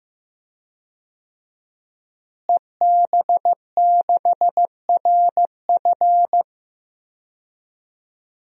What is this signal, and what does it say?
Morse code, heard as a keyed tone, about 15 wpm, 700 Hz. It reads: EB6RF
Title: DXCC-100 Call Signs - 5 Characters - Repeated 3x - 15wpm